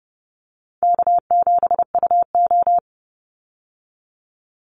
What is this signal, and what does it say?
Morse code, heard as a keyed tone, about 30 wpm, 700 Hz. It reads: K7UO
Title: USA Call Signs 30wpm